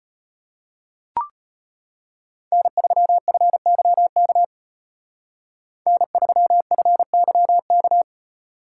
Morse code, heard as a keyed tone, about 33 wpm, 700 Hz. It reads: N3FYK N3FYK